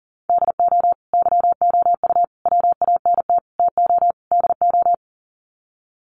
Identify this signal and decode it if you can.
Morse code, heard as a keyed tone, about 40 wpm, 700 Hz. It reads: DO YOU WANT TO DO
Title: Sets of 5 Words Encore 40wpm